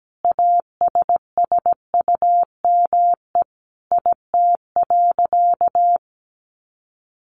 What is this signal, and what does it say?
Morse code, heard as a keyed tone, about 17 wpm, 700 Hz. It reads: ASSUME IT.
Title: Sentences from Top 1500 Words 17wpm